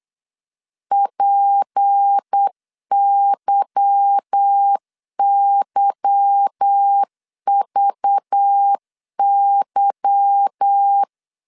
Morse code, heard as a keyed tone, about 8 wpm, 800 Hz. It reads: PYYVY
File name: LEZIONI / 8 YJ45 / 8 Y 8wpm 3245